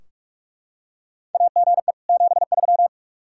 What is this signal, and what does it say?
Morse code, heard as a keyed tone, about 43 wpm, 700 Hz. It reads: AGE 73
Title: Practice QSO - Exchange 3 - More Info - 45wpm